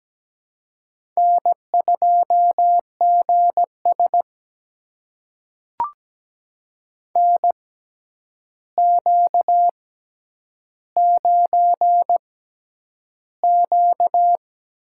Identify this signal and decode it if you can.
Morse code, heard as a keyed tone, about 17 wpm, spaced as for 9 wpm, 700 Hz. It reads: N2GS NQ9Q
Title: USA Call Signs - 4 Characters - 6x Character Spacing - 17wpm